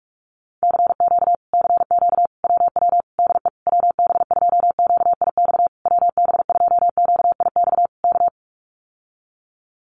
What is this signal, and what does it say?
Morse code, heard as a keyed tone, about 44 wpm, 700 Hz. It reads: CQ CQ WW DE WB2QIX WB2QIX K
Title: Common QSO Phrases Encore 45wpm